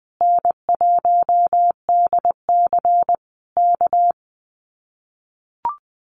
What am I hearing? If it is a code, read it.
Morse code, heard as a keyed tone, about 20 wpm, 700 Hz. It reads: N1DC K
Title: Common QSO Phrases Encore 20wpm